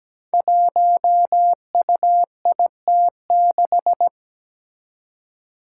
Morse code, heard as a keyed tone, about 17 wpm, 700 Hz. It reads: 1UIT6
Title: Five Challenging Letter-Number Combinations 17wpm